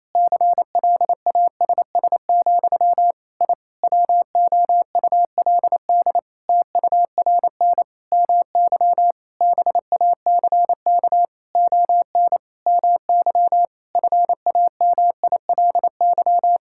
Morse code, heard as a keyed tone, about 28 wpm, 700 Hz. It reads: CLASS, I WOULD TURN MY BACK ON MY FAMILY
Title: Sentences from Top 1000 Words - Review Part 6 - 28wpm